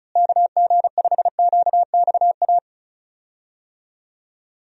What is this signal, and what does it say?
Morse code, heard as a keyed tone, about 34 wpm, 700 Hz. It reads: KG5QXA
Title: USA Call Signs - 6 Characters - Repeated 3x - 35wpm